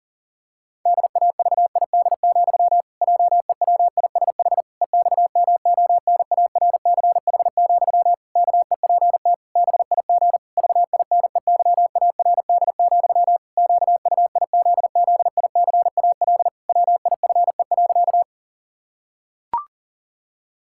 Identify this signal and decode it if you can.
Morse code, heard as a keyed tone, about 37 wpm, 700 Hz. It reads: DAVID, JEWISH EXMONARCH, KEPT BIG VINEYARD, QUIZZICAL WIFE.
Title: Pangram Challenge v2 - 40wpm